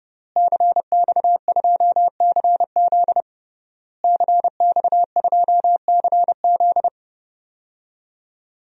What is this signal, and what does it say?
Morse code, heard as a keyed tone, about 32 wpm, 700 Hz. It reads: CX2CZ CX2CZ
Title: DXCC-100 Call Signs - 5 Characters - Repeated 3x - 30wpm